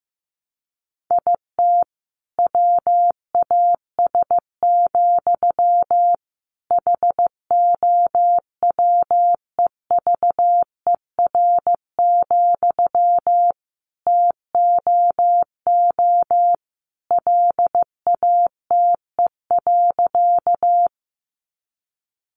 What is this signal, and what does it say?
Morse code, heard as a keyed tone, about 15 wpm, 700 Hz. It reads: IT WAS, HOWEVER, TOO LATE.
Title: Sentences from Top 600 Words 15wpm